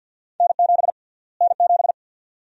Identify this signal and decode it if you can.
Morse code, heard as a keyed tone, about 46 wpm, 700 Hz. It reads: N7 N7
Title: USA Call Sign Prefixes - Repeated 2x - 50wpm